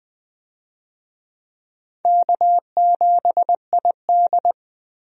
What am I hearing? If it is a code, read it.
Morse code, heard as a keyed tone, about 21 wpm, 700 Hz. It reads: K7ID
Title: USA Call Signs - 4 Characters - Repeated 2x - 20wpm